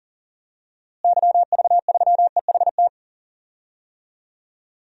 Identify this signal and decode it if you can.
Morse code, heard as a keyed tone, about 37 wpm, 700 Hz. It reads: YV3EHT
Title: DXCC-100 Call Signs - 6 Characters - Repeated 3x - 40wpm